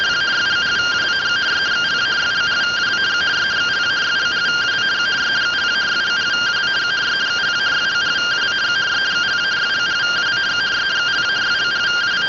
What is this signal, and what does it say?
Two-tone teleprinter signal, about 170 Hz shift.